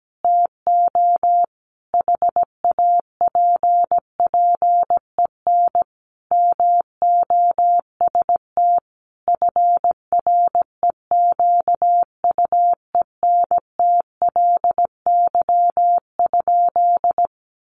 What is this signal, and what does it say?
Morse code, heard as a keyed tone, about 17 wpm, 700 Hz. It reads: TO HAPPEN MOST FREQUENTLY?
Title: Sentences from Top 1300 Words 17wpm